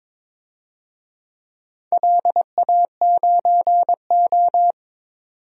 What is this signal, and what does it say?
Morse code, heard as a keyed tone, about 22 wpm, 700 Hz. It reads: LA9O
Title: DXCC-100 Call Signs - 4 Characters - 22wpm